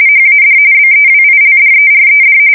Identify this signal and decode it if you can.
RTTY, about 45.45 baud. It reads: HAD BEWILDERED
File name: fsk170 / USS-New-Jersey